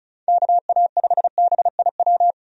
Morse code, heard as a keyed tone, about 34 wpm, 700 Hz. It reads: KA5BIW
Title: USA Call Signs - 6 Characters - 35wpm